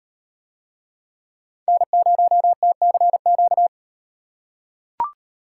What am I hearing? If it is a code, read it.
Morse code, heard as a keyed tone, about 38 wpm, 700 Hz. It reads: N0TCQ